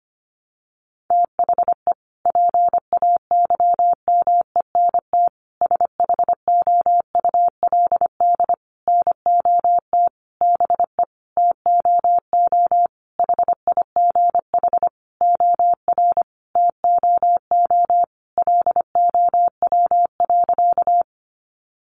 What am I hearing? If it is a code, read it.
Morse code, heard as a keyed tone, about 25 wpm, 700 Hz. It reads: THE PAYMENT SHOULD NOT BE TOO HIGH OR TOO LOW.
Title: Sentences from Top 1200 Words 25wpm